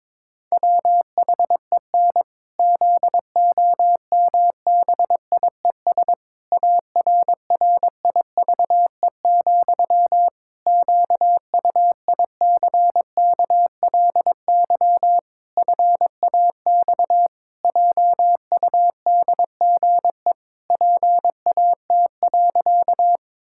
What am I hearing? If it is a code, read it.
Morse code, heard as a keyed tone, about 22 wpm, 700 Hz. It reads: WHEN ZOMBIES ARRIVE, QUICKLY FAX JUDGE PAT.
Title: Pangram Challenge v2 - 22wpm